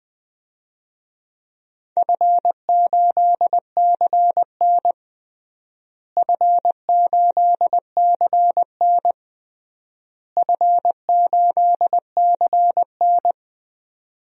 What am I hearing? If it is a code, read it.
Morse code, heard as a keyed tone, about 20 wpm, 700 Hz. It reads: F8CN F8CN F8CN
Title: DXCC-100 Call Signs - 4 Characters - Repeated 3x - 20wpm